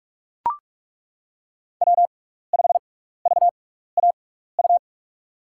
Morse code, heard as a keyed tone, about 42 wpm, spaced as for 15 wpm, 700 Hz. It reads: W5VAU
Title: USA Call Signs - 5 Characters - 6x Character Spacing - 45wpm